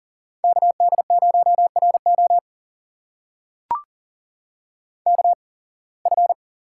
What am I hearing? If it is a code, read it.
Morse code, heard as a keyed tone, about 39 wpm, spaced as for 26 wpm, 700 Hz. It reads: KD0RO KF
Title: USA Call Signs - 5 Characters - 8x Character Spacing - 40wpm